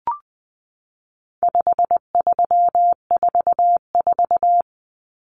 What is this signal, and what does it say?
Morse code, heard as a keyed tone, about 20 wpm, 700 Hz. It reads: 5344
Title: Numbers - 4-Digit 20wpm